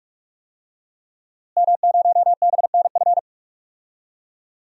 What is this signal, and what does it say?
Morse code, heard as a keyed tone, about 46 wpm, 700 Hz. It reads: M0BNF
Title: DXCC-100 Call Signs - 5 Characters - Repeated 2x - 45wpm